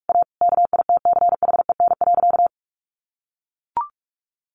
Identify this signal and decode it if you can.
Morse code, heard as a keyed tone, about 44 wpm, 700 Hz. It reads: A KITCHEN.
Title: Sentences from Top 1500 Words 45wpm